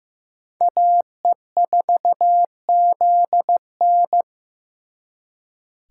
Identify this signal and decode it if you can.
Morse code, heard as a keyed tone, about 18 wpm, 700 Hz. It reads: AE4ZN
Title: Speed Racing - USA 2x2 Call Signs 15wpm